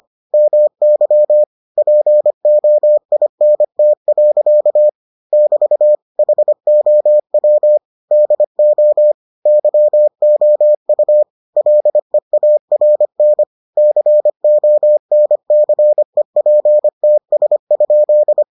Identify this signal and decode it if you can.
Morse code, heard as a keyed tone, about 25 wpm, 600 Hz. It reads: MY POINT. <BT> HOW DO YOU LEARN CONCEPTS?